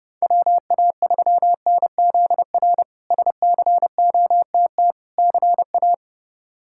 Morse code, heard as a keyed tone, about 32 wpm, 700 Hz. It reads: WA3NZR SCOTT CA
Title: K1USN SST QSO Practice - 30wpm